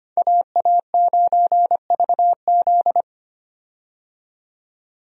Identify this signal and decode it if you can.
Morse code, heard as a keyed tone, about 25 wpm, 700 Hz. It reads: AA9VZ
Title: USA Call Signs - 5 Characters - 25wpm